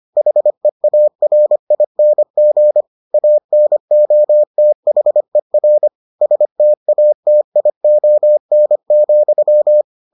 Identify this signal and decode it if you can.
Morse code, heard as a keyed tone, about 25 wpm, 600 Hz. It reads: HEARING ANOTHER STATION,